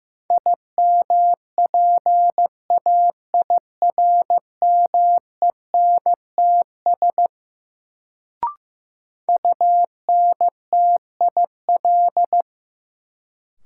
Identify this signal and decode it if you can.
Morse code, heard as a keyed tone, about 15 wpm, 700 Hz. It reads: IMPAIRMENTS UNTIL
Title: Sets of 5 Words Encore - New Word Drill - 15wpm